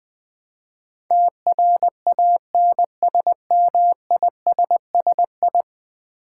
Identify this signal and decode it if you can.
Morse code, heard as a keyed tone, about 20 wpm, 700 Hz. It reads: TRANSMISSI
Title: Build A Word - 5 Letters At A Time - 12-Letter Words - 20wpm